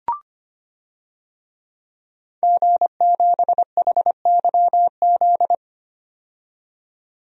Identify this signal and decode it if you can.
Morse code, heard as a keyed tone, about 25 wpm, 700 Hz. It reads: G7HYZ
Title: DXCC-100 Call Signs - 5 Characters - 25wpm